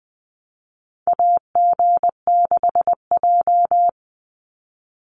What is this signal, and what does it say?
Morse code, heard as a keyed tone, about 20 wpm, 700 Hz. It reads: AG6J
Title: USA Call Signs 20wpm